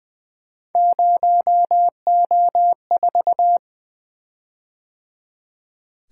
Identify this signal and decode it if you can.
Morse code, heard as a keyed tone, about 20 wpm, 700 Hz. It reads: 0O4
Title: Three Letter-Number Combinations - 20wpm